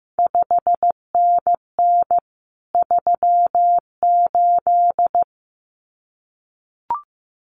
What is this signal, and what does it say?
Morse code, heard as a keyed tone, about 15 wpm, 700 Hz. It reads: 5NN 38
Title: CQ WW DX Contest Exchange Encore - 15wpm